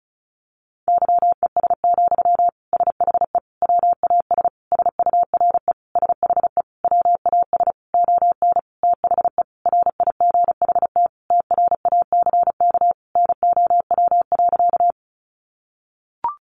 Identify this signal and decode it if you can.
Morse code, heard as a keyed tone, about 34 wpm, 700 Hz. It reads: YES, SHE WAS SURE SHE WAS ON THE RIGHT TRACK NOW.
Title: Sentences from Top 1100 Words 35wpm